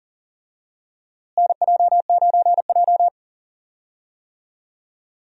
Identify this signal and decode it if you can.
Morse code, heard as a keyed tone, about 40 wpm, 700 Hz. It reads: NJ9J
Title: USA Call Signs - 4 Characters - Repeated 2x - 40wpm